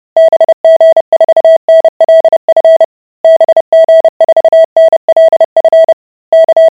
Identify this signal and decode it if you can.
Morse code, heard as a keyed tone, about 30 wpm, 650 Hz. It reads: BG4NLF BG4NLF K